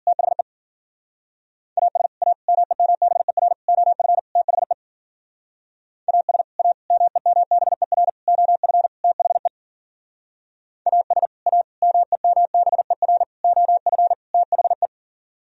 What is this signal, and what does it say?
Morse code, heard as a keyed tone, about 48 wpm, 700 Hz. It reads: THE AS A MEMBER OF THE AS A MEMBER OF THE AS A MEMBER OF THE
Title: Speed-Racing - Sets of 5 Words 40wpm